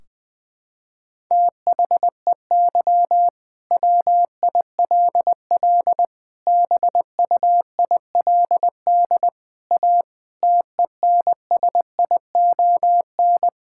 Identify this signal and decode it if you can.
Morse code, heard as a keyed tone, about 21 wpm, 700 Hz. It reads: THEY WILL BUILD A TENSION
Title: Sentences from Top 1600 Words 20wpm